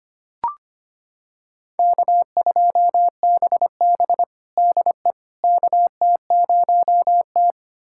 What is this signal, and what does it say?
Morse code, heard as a keyed tone, about 25 wpm, 700 Hz. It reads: K2BB DE KT0T
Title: USA Callsign DE Callsign - 4 Characters - 25wpm